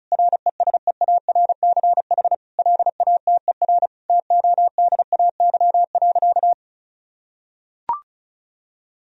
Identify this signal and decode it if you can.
Morse code, heard as a keyed tone, about 36 wpm, 700 Hz. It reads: RESEARCH LATER TODAY.
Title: Sentences from Top 1000 Words - Review Part 4 - 35wpm